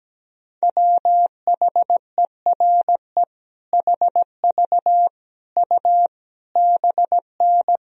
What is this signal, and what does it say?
Morse code, heard as a keyed tone, about 17 wpm, 700 Hz. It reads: WHERE HV U BN
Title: Sentences from Top 300 Words and Abbreviations 17wpm